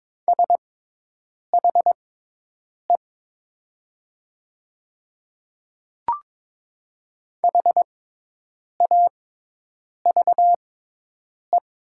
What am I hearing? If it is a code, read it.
Morse code, heard as a keyed tone, about 22 wpm, spaced as for 8 wpm, 700 Hz. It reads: SHE HAVE
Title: 6x Character Spacing - Top 100 Words - 22wpm